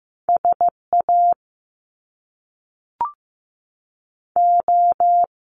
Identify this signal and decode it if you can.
Morse code, heard as a keyed tone, about 15 wpm, 700 Hz. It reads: SA O
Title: Top Two Letter Combinations in Words 15wpm